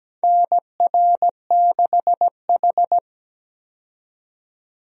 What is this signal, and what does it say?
Morse code, heard as a keyed tone, about 17 wpm, 700 Hz. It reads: NR6H
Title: USA Call Signs - 4 Characters - 17wpm